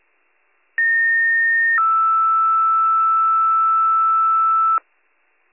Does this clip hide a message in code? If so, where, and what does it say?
Yes: two-tone page from 0.8 to 4.8 s, heard as a one-second tone, then a three-second tone, about 1850 then 1320 Hz, with noise.